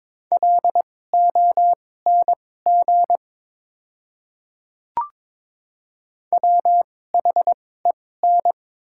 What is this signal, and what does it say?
Morse code, heard as a keyed tone, about 22 wpm, spaced as for 16 wpm, 700 Hz. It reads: LONG WHEN
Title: 2x Character Spacing - Top 100 Words - 22wpm